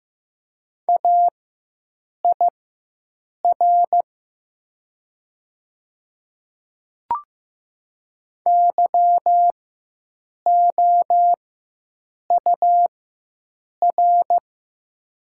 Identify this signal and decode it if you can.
Morse code, heard as a keyed tone, about 15 wpm, spaced as for 7 wpm, 700 Hz. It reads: AIR YOUR
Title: 4x Character Spacing - Top 200 Words - 15wpm